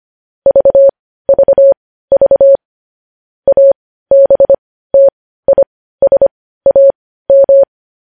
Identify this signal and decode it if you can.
Morse code, heard as a keyed tone, about 25 wpm, spaced as for 15 wpm, 550 Hz. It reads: VVV ABTISAM
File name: podcast / News headlines-1x-25x15